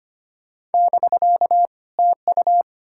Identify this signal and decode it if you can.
Morse code, heard as a keyed tone, about 33 wpm, 700 Hz. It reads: <BK> TU